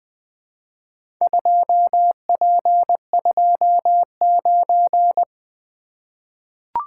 Morse code, heard as a keyed tone, about 20 wpm, 700 Hz. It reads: 2P29
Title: Four Letter-Number Combinations - 20wpm